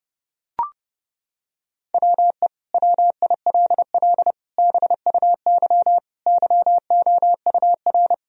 Morse code, heard as a keyed tone, about 30 wpm, 700 Hz. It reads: WE WILL BUY YOUR
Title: Sentences from Top 1000 Words - Review Part 4 - 30wpm